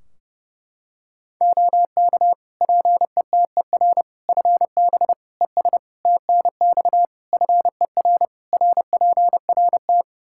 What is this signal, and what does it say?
Morse code, heard as a keyed tone, about 32 wpm, 700 Hz. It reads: OK PETER FB ES TNX FER RPRT
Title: K1USN SST QSO Practice 30wpm